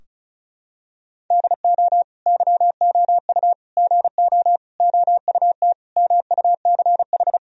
Morse code, heard as a keyed tone, about 36 wpm, 700 Hz. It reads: DO YOU GO OUT MUCH
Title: Sentences from Top 1000 Words - Review Part 4 - 35wpm